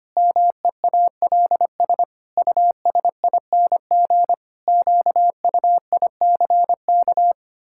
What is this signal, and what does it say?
Morse code, heard as a keyed tone, about 25 wpm, 700 Hz. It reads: MEALS USING QUICK
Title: Pangram Challenge v2 - 25wpm